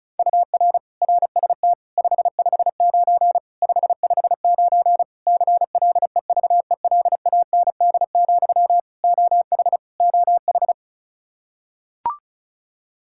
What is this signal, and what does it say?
Morse code, heard as a keyed tone, about 36 wpm, 700 Hz. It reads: UR RST 559 559 CLEVELAND, OH OH
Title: Practice QSO - Exchange 1 - Beginning - 35wpm